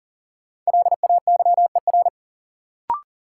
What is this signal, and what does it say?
Morse code, heard as a keyed tone, about 37 wpm, 700 Hz. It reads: LAYER
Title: Sentences from Top 1600 Words - New Word Drill - 40wpm